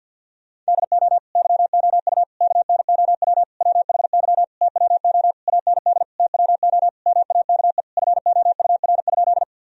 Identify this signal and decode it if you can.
Morse code, heard as a keyed tone, about 48 wpm, 700 Hz. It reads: DO YOU KNOW WHY TWO AND TWO MAKE FOUR?
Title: Sentences from Top 1000 Words - Review Part 1 - 50wpm